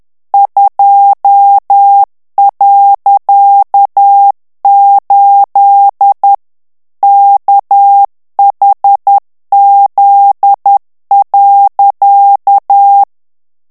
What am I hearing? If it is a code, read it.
Morse code, heard as a keyed tone, about 10 wpm, 800 Hz. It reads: 2.8 KHZ.
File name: ANGLAIS texte clair 3 vitesse 10 mots minute